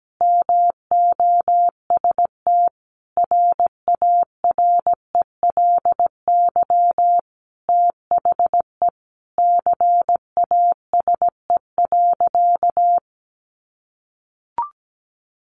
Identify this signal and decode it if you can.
Morse code, heard as a keyed tone, about 17 wpm, 700 Hz. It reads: MOST RARELY THE CASE.
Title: Sentences from Top 1400 Words 17wpm